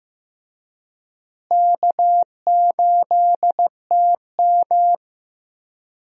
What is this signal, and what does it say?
Morse code, heard as a keyed tone, about 15 wpm, 700 Hz. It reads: K8TM
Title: USA Call Signs - 4 Characters - Repeated 2x - 15wpm